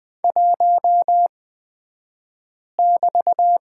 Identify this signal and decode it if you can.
Morse code, heard as a keyed tone, about 20 wpm, spaced as for 5 wpm, 700 Hz. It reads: 1<BT>
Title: 223 Lesson 36d - BT - Two Characters 20wpm